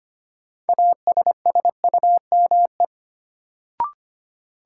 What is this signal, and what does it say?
Morse code, heard as a keyed tone, about 25 wpm, 700 Hz. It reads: ASSUME
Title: Sentences from Top 1500 Words - New Word Drill Alt - 25wpm